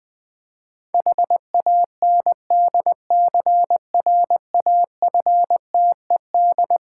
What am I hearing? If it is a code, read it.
Morse code, heard as a keyed tone, about 20 wpm, 690 Hz. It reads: HANDCRAFTED
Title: Difficult Words with Hints - 20wpm